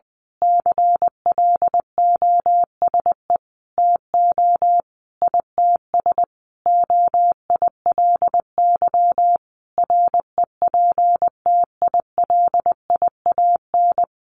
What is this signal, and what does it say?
Morse code, heard as a keyed tone, about 20 wpm, 700 Hz. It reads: CLOSE TO ITS OILY REPTILIAN
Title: A Sound Of Thunder 20wpm